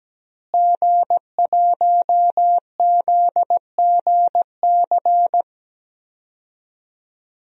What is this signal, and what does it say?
Morse code, heard as a keyed tone, about 17 wpm, 700 Hz. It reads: G1ZGC
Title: DXCC-100 Call Signs - 5 Characters - Repeated 3x - 17wpm